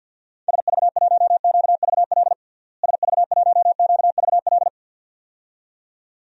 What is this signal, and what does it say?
Morse code, heard as a keyed tone, about 51 wpm, 700 Hz. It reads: SV1QVL SV1QVL
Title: DXCC-100 Call Signs - 6 Characters - Repeated 2x - 50wpm